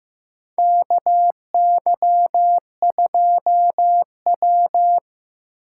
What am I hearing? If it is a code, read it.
Morse code, heard as a keyed tone, about 15 wpm, 700 Hz. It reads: KY2W
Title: USA Call Signs - 2x Character Spacing - 15wpm